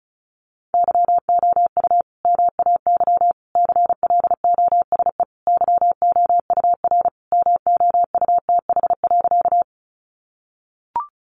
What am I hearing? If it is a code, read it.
Morse code, heard as a keyed tone, about 34 wpm, 700 Hz. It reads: YOU MAY CLOSE YOUR MOUTH.
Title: Sentences from Top 1500 Words 35wpm